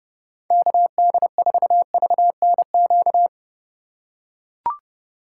Morse code, heard as a keyed tone, about 30 wpm, 700 Hz. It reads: KD4VNQ
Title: USA Call Signs - 6 Characters - 8x Character Spacing - 30wpm